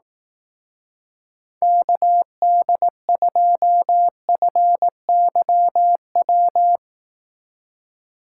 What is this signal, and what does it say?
Morse code, heard as a keyed tone, about 18 wpm, 700 Hz. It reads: KD2FYW
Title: Speed Racing - USA 2x3 Call Signs 15wpm